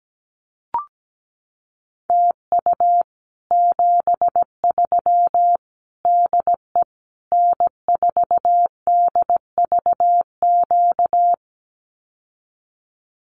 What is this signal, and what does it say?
Morse code, heard as a keyed tone, about 17 wpm, 700 Hz. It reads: TU 73 DE N4DVQ
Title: Common QSO Phrases Encore 17wpm